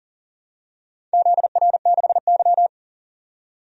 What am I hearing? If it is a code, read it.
Morse code, heard as a keyed tone, about 40 wpm, 700 Hz. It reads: ZR6Y
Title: DXCC-100 Call Signs - 4 Characters - 40wpm